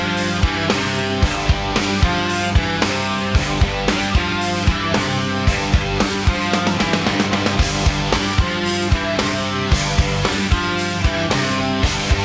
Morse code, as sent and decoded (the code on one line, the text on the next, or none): none